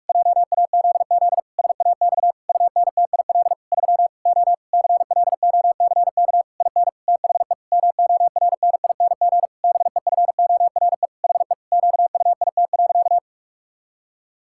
Morse code, heard as a keyed tone, about 46 wpm, 700 Hz. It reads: JAZZ SAX UNTIL 3 O CLOCK IN THE MORNING BEFORE HE QUIT.
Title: Pangram Challenge v2 - 45wpm